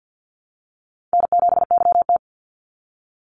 Morse code, heard as a keyed tone, about 50 wpm, 700 Hz. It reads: N7YT
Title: Call Signs Repeated 2x - 50wpm